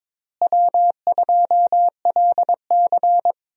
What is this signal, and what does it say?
Morse code, heard as a keyed tone, about 22 wpm, 700 Hz. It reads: W2LC
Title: USA Call Signs - 4 Characters - Repeated 2x - 22wpm